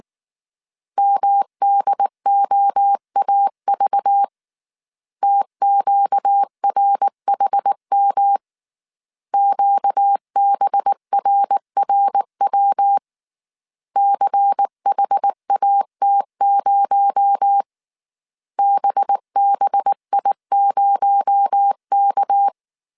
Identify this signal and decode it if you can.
Morse code, heard as a keyed tone, about 19 wpm, 800 Hz. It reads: MDOAV TQRHM QBRRW CHAT0 BBI0K